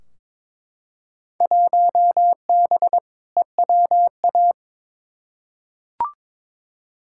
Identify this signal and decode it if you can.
Morse code, heard as a keyed tone, about 22 wpm, 700 Hz. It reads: 1B EWA
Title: Field Day Exchanges Encore - 22wpm